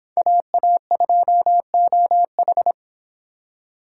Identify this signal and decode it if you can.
Morse code, heard as a keyed tone, about 26 wpm, 700 Hz. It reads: AA2OH